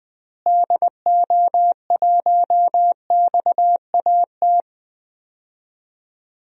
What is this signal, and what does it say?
Morse code, heard as a keyed tone, about 20 wpm, 700 Hz. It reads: DO1XAT
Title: DXCC-100 Call Signs - 6 Characters - Repeated 2x - 20wpm